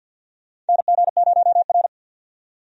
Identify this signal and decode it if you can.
Morse code, heard as a keyed tone, about 49 wpm, 700 Hz. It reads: NG0R
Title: USA Call Signs - 4 Characters - 50wpm